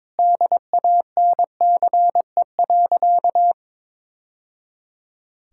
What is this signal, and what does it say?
Morse code, heard as a keyed tone, about 22 wpm, 700 Hz. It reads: DANCE.
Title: Sentences from Top 1000 Words - Review Part 6 - 22wpm